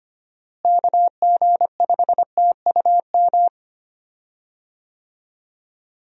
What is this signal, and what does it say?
Morse code, heard as a keyed tone, about 25 wpm, 700 Hz. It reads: KG5TUM
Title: USA Call Signs - 6 Characters - Repeated 3x - 25wpm